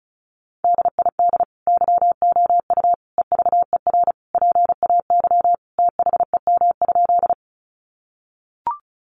Morse code, heard as a keyed tone, about 34 wpm, 700 Hz. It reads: DID YOU EVER PAY THEM?
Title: Sentences from Top 700 Words 35wpm